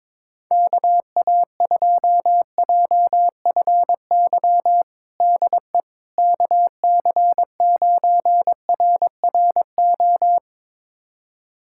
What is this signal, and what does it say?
Morse code, heard as a keyed tone, about 22 wpm, 700 Hz. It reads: KA2JFY DE KC9RRO
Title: USA Callsign DE Callsign - 6 Characters - 22wpm